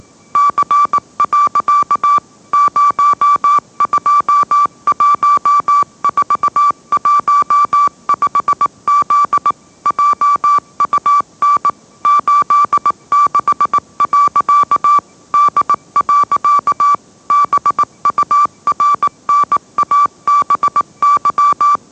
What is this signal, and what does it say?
Morse code, heard as a keyed tone, about 18 wpm, 1160 Hz. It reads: C. 021415Z JUN 86. D. BURNABY
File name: mrx / mrx082 20wpm